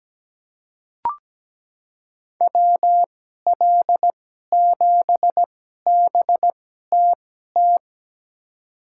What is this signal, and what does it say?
Morse code, heard as a keyed tone, about 17 wpm, spaced as for 12 wpm, 700 Hz. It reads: WL7BTT
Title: USA Call Signs - 6 Characters - 2x Character Spacing - 17wpm